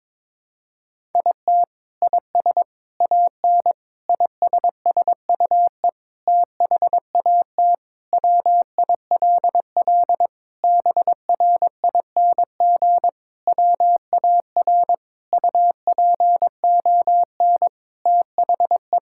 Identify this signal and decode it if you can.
Morse code, heard as a keyed tone, about 22 wpm, 700 Hz. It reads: IT IS AN ISSUE THAT WILL BRING WAR UPON THE
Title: Sentences from Top 1000 Words - Review Part 6 - 22wpm